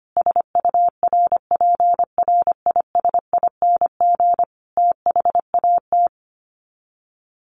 Morse code, heard as a keyed tone, about 25 wpm, 700 Hz. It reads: SURPRISING THAT
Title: Sets of 5 Words 25wpm